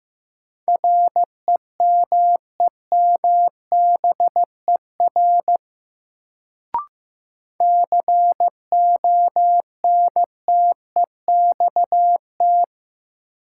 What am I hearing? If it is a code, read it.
Morse code, heard as a keyed tone, about 16 wpm, 700 Hz. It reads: REMEMBER CONTEXT
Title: Sets of 4 Words - New Word Drill - 15wpm